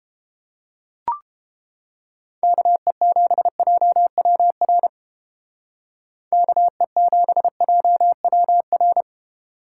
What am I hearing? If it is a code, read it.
Morse code, heard as a keyed tone, about 29 wpm, 700 Hz. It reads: KE7JWR KE7JWR